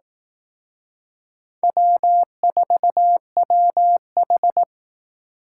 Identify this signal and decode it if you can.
Morse code, heard as a keyed tone, about 18 wpm, 700 Hz. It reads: W4WH
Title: Speed Racing - USA 1x2 Call Signs 15wpm